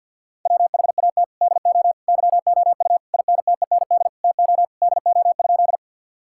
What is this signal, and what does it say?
Morse code, heard as a keyed tone, about 50 wpm, 700 Hz. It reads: WHAT DO YOU INTEND TO DO?
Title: Sentences from Top 1600 Words 50wpm